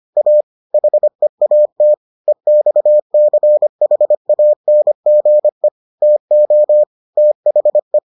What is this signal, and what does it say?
Morse code, heard as a keyed tone, about 25 wpm, 600 Hz. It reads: A HEAT EXCHANGE TO THE